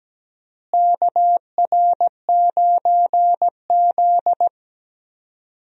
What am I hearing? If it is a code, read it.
Morse code, heard as a keyed tone, about 17 wpm, 700 Hz. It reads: KR9Z
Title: USA Call Signs - 4 Characters - Repeated 3x - 17wpm